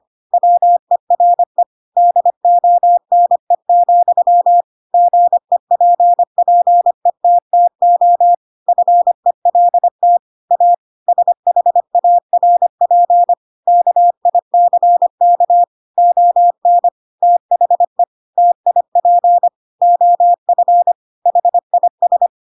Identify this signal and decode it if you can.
Morse code, heard as a keyed tone, about 25 wpm, 700 Hz. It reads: WERE DONE, GEPPETTO FELT A SHARP KICK ON THE TIP OF HIS
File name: Pinocchio / Pinocchio ch3 25 25 ik1hge